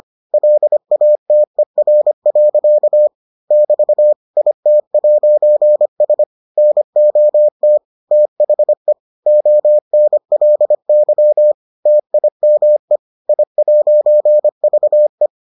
Morse code, heard as a keyed tone, about 25 wpm, 600 Hz. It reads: LATER. <BT> IT'S NOT THE ONLY TIME I'VE